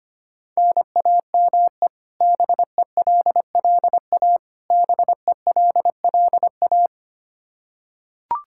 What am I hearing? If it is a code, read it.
Morse code, heard as a keyed tone, about 25 wpm, 700 Hz. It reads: NAME BELLA BELLA
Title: Practice QSO - Exchange 1 - Beginning - 25wpm